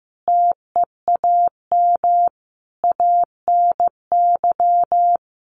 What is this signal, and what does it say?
Morse code, heard as a keyed tone, about 15 wpm, 700 Hz. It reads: TEAM ANY
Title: Sentences from Top 1500 Words 15wpm